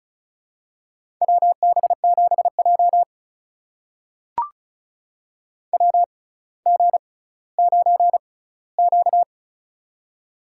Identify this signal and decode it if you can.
Morse code, heard as a keyed tone, about 36 wpm, spaced as for 18 wpm, 700 Hz. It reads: WB7J WG9Q
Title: USA Call Signs - 4 Characters - 6x Character Spacing - 35wpm